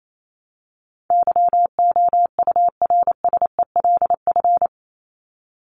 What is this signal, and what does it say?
Morse code, heard as a keyed tone, about 28 wpm, 700 Hz. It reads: YOURSELF
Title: Top 501-750 Words 28wpm